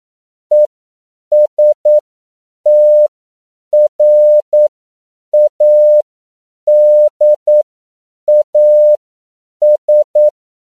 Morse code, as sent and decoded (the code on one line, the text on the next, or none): . ... - .-. .- -.. .- ...
ESTRADAS